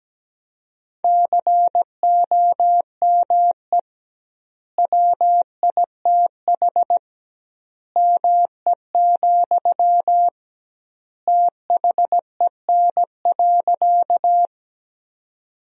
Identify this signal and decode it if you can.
Morse code, heard as a keyed tone, about 17 wpm, 700 Hz. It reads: COME WITH ME, THEN.
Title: 2x Word-Spacing - Sentences from Top 200 Words - 17wpm